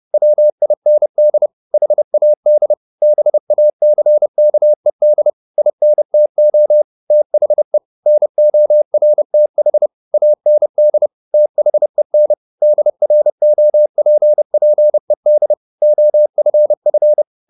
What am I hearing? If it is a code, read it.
Morse code, heard as a keyed tone, about 30 wpm, 600 Hz. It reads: WIND HAD BACKED INTO THE NORTH AND THEN DROPPED OFF
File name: the old man and the sea / the old man and the sea 1 30wpm